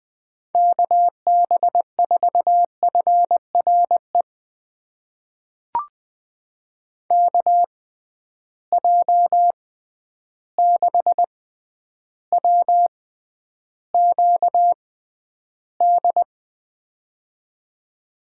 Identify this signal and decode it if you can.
Morse code, heard as a keyed tone, about 21 wpm, spaced as for 10 wpm, 700 Hz. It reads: KB4FRE KJ6WQD
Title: USA Call Signs - 6x Character Spacing - 20wpm